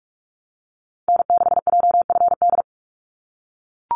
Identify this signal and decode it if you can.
Morse code, heard as a keyed tone, about 44 wpm, 700 Hz. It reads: N6JFD
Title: NA SOTA Activator Callsigns 45wpm